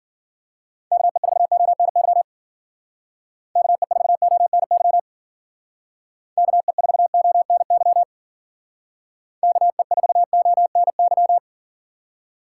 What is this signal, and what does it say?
Morse code, heard as a keyed tone, about 52 wpm, 700 Hz. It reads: KE4ONY KE4ONY KE4ONY KE4ONY